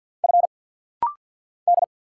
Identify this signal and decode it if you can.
Morse code, heard as a keyed tone, about 52 wpm, 700 Hz. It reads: F D
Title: Single Letters - Rapid-Fire 50wpm